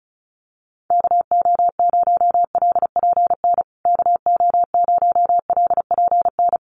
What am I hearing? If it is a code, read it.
Morse code, heard as a keyed tone, about 34 wpm, 700 Hz. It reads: KO0LPN KO0LPN
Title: USA Call Signs Repeated 3x 35wpm